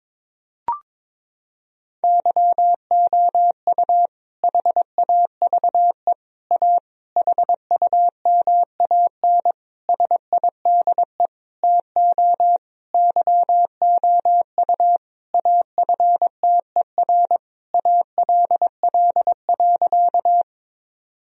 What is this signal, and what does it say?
Morse code, heard as a keyed tone, about 22 wpm, 700 Hz. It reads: YOU HAVE A HUMAN SIDE TO YOU AFTER ALL.
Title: Sentences from Top 1000 Words - Review Part 3 - 22wpm